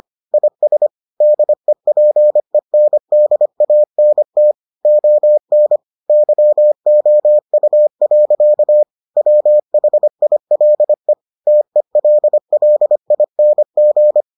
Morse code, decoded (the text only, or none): IS DEPENDANT ON YOU. WHILE TELLING